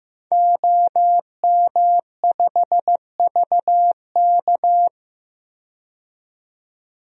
Morse code, heard as a keyed tone, about 15 wpm, 700 Hz. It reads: OM5VK
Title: DXCC-100 Call Signs - 5 Characters - 15wpm